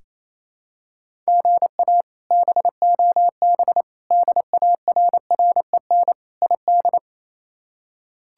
Morse code, heard as a keyed tone, about 28 wpm, 700 Hz. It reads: GA BOB DARREN ID
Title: K1USN SST QSO Practice - 28wpm